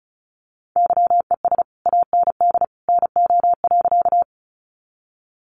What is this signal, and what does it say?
Morse code, heard as a keyed tone, about 34 wpm, 700 Hz. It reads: YES AND NO.
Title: Sentences from Top 500 Words 35wpm